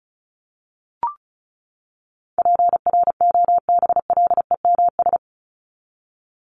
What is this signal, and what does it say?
Morse code, heard as a keyed tone, about 34 wpm, 700 Hz. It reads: PROBLEMS
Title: ES:S Suffix - Top 100 Words with Base Words 35wpm